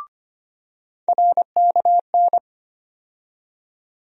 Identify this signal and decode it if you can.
Morse code, heard as a keyed tone, about 25 wpm, 700 Hz. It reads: RKN
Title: USA Call Sign Postfixes - 3 Letters 25wpm